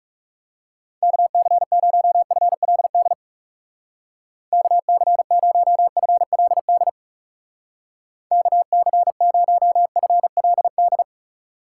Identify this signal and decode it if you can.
Morse code, heard as a keyed tone, about 36 wpm, 700 Hz. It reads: KC0FLD KC0FLD KC0FLD
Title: Speed Racing - USA 2x3 Call Signs 30wpm